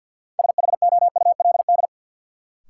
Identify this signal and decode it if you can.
Morse code, heard as a keyed tone, about 52 wpm, 700 Hz. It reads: SHOULD
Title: Sentences from Top 300 Words - New Word Drill - 50wpm